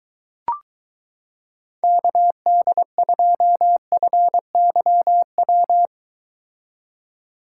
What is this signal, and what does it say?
Morse code, heard as a keyed tone, about 23 wpm, 700 Hz. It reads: KD2FYW